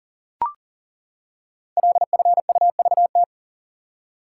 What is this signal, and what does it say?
Morse code, heard as a keyed tone, about 39 wpm, 700 Hz. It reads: LFUVT
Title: Code Groups - 5 Letters - 40wpm